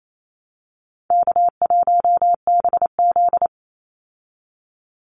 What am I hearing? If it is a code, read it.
Morse code, heard as a keyed tone, about 28 wpm, 700 Hz. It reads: K1BZ
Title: Call Signs Repeated 2x - 28wpm